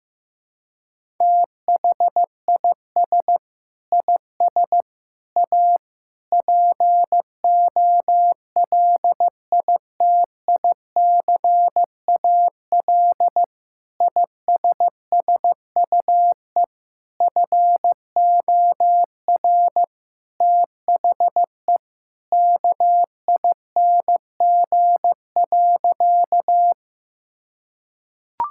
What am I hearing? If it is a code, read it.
Morse code, heard as a keyed tone, about 15 wpm, 700 Hz. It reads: THIS IS A POLITICAL ISSUE FOR THE KING.
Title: Sentences from Top 1000 Words - Review Part 6 - 15wpm